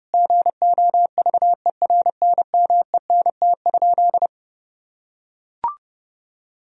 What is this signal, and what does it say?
Morse code, heard as a keyed tone, about 30 wpm, 700 Hz. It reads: GOVERNMENT?
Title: Sentences from Top 1000 Words - Review Part 4 - 30wpm